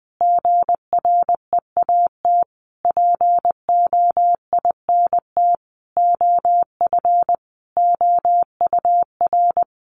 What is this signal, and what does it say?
Morse code, heard as a keyed tone, about 20 wpm, 700 Hz. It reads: GREAT POINT OF OUR
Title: Sentences from Top 1500 Words 20wpm